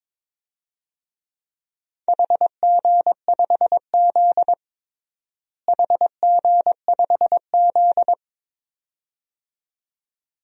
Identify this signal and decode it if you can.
Morse code, heard as a keyed tone, about 21 wpm, 700 Hz. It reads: HG5Z HG5Z
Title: DXCC-100 Call Signs - 4 Characters - Repeated 2x - 22wpm